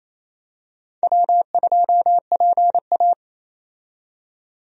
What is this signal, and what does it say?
Morse code, heard as a keyed tone, about 27 wpm, 700 Hz. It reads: W2PA
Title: USA Call Signs - 4 Characters - 28wpm